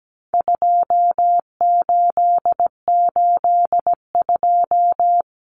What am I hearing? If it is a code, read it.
Morse code, heard as a keyed tone, about 17 wpm, 700 Hz. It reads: 2882
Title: Common Cut Numbers - 4-Digit 17wpm